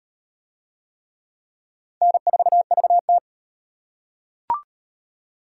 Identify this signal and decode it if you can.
Morse code, heard as a keyed tone, about 35 wpm, 700 Hz. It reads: N4VT